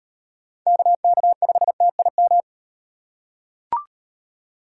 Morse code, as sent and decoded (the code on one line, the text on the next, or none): -.- -.- ..... - .. --
KK5TIM